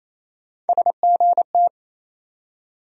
Morse code, heard as a keyed tone, about 28 wpm, 700 Hz. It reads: SGT
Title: USA Call Sign Postfixes - 3 Letters 28wpm